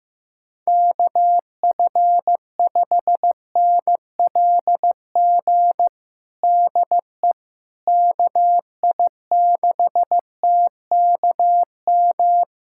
Morse code, heard as a keyed tone, about 15 wpm, 700 Hz. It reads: KF5NLG DE KI6TKM
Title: USA Callsign DE Callsign - 6 Characters - 15wpm